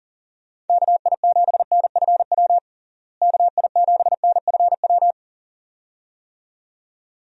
Morse code, heard as a keyed tone, about 37 wpm, 700 Hz. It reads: KI7NFW KI7NFW
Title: USA Call Signs - 6 Characters - Repeated 2x - 40wpm